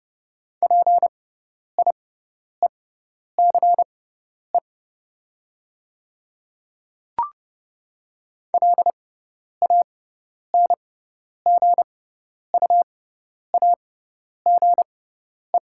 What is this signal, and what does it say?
Morse code, heard as a keyed tone, about 30 wpm, spaced as for 10 wpm, 700 Hz. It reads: PIECE LANGUAGE
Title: 6x Character Spacing - Top 500 Words - 30wpm